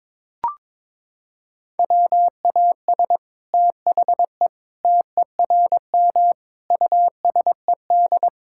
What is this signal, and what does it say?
Morse code, heard as a keyed tone, about 22 wpm, 700 Hz. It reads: WAS THE TERM USED
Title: Sentences from Top 1000 Words - Review Part 6 - 22wpm